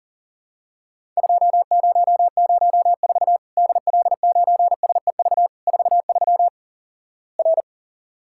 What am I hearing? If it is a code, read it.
Morse code, heard as a keyed tone, about 43 wpm, 700 Hz. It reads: 2004 DL9SEV 43 R
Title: Worked All Europe DX Contest - QTC Exchange - 40wpm